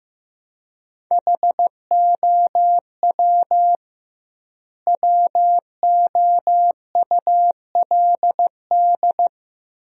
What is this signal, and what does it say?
Morse code, heard as a keyed tone, about 16 wpm, 700 Hz. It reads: HOW WOULD
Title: 2x Word-Spacing - Sentences from Top 100 Words - 15wpm